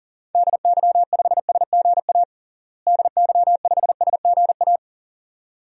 Morse code, heard as a keyed tone, about 40 wpm, 700 Hz. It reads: DY5SGA DY5SGA
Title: DXCC-100 Call Signs - 6 Characters - Repeated 2x - 40wpm